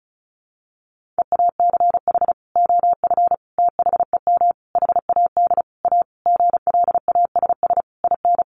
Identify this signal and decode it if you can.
Morse code, heard as a keyed tone, about 34 wpm, 700 Hz. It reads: EACH OF THEM HAD A GLASS IN
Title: Sentences from Top 1300 Words 35wpm